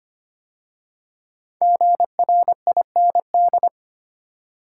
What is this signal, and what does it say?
Morse code, heard as a keyed tone, about 25 wpm, 700 Hz. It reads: GRIND
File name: 5 Letter Words - First 3 Letters Given - 25wpm